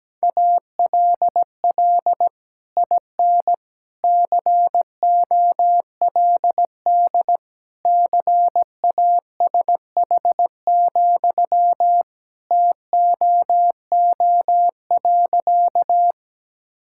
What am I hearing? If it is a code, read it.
Morse code, heard as a keyed tone, about 17 wpm, 700 Hz. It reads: ALL IN COLD CASH, TOO.
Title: Sentences from Top 1000 Words - Review Part 5 - 17wpm